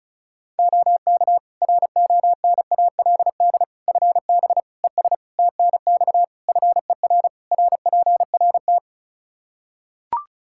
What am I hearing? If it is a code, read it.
Morse code, heard as a keyed tone, about 36 wpm, 700 Hz. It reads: OK RONALD FB ES TNX FER RPRT
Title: Practice QSO - Exchange 2 - Info - 35wpm